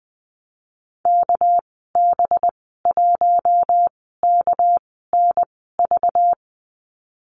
Morse code, heard as a keyed tone, about 20 wpm, spaced as for 14 wpm, 700 Hz. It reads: KB1KNV
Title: 231 Lesson 37g - Letter x - Call Signs 20wpm